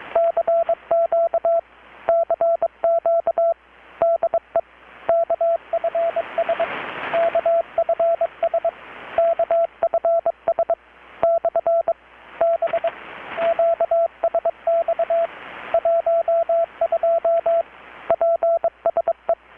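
Morse code, heard as a keyed tone, about 22 wpm, 670 Hz, with radio noise. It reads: CQ CQ DE KFS KFS KFS / B QSX 12 PSE
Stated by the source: Morse from KPH's Night of Nights XX maritime radiotelegraphy event, received in Chiapas, Mexico about 3,800 km away on a Kenwood R-600 with a grounded 10 m long-wire antenna.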